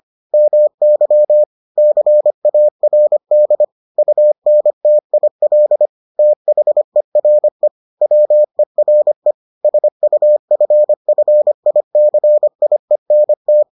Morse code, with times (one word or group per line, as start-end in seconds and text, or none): 0.3-1.4 MY
1.8-3.6 CARD
4.0-5.9 UNTIL
6.2-7.7 THERE
8.0-9.3 WERE
9.6-13.6 SUFFICIENT